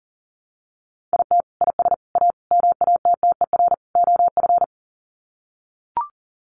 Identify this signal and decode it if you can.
Morse code, heard as a keyed tone, about 40 wpm, 700 Hz. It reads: IT IS A MATTER OF
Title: Sets of 5 Words Encore2 40wpm